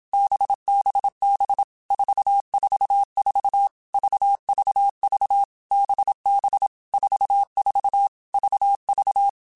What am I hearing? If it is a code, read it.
Morse code, heard as a keyed tone, about 27 wpm, 800 Hz. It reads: BBB 444 VVV BB 44 VV
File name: Basic Letter Groups By7Groups / VB46 / VB46@25